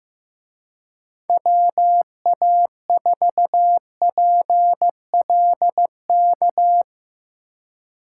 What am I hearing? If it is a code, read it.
Morse code, heard as a keyed tone, about 18 wpm, 700 Hz. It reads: WA4PLK
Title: Speed Racing - USA 2x3 Call Signs 15wpm